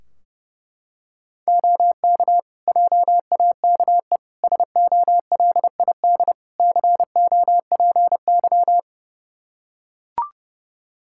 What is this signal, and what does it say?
Morse code, heard as a keyed tone, about 32 wpm, 700 Hz. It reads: OK JAKE SOLID COPY
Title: Practice QSO - Exchange 3 - More Info - 30wpm